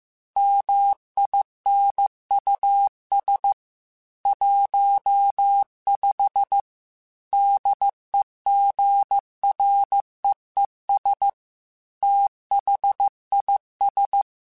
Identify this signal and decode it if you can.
Morse code, heard as a keyed tone, about 15 wpm, 800 Hz. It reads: MINUS 15 DEGREES THIS